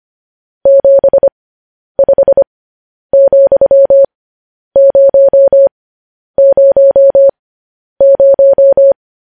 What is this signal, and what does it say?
Morse code, heard as a keyed tone, about 25 wpm, spaced as for 10 wpm, 550 Hz. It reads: 75,000